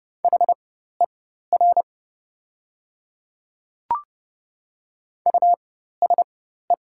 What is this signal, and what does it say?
Morse code, heard as a keyed tone, about 30 wpm, spaced as for 14 wpm, 700 Hz. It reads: HER USE
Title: 4x Character Spacing - Top 100 Words - 30wpm